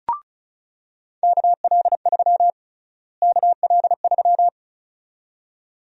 Morse code, heard as a keyed tone, about 36 wpm, 700 Hz. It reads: KL3 KL3
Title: USA Call Sign Prefixes - Repeated 2x - 35wpm